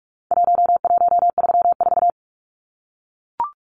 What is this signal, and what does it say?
Morse code, heard as a keyed tone, about 44 wpm, 700 Hz. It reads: 1134
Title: Numbers - Contest Style - 4-Digit 45wpm